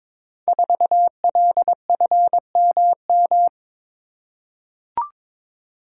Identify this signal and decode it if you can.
Morse code, heard as a keyed tone, about 22 wpm, 700 Hz. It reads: 4LFMM
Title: Code Groups - 5 Letter:Numbers - 22wpm